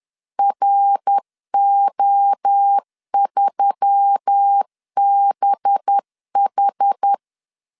Morse code, heard as a keyed tone, about 10 wpm, 800 Hz. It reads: RO3BH
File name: LEZIONI / 9 2867 10wpm 2733